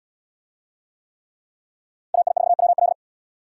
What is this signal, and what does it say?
Morse code, heard as a keyed tone, about 71 wpm, 700 Hz. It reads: AI4WL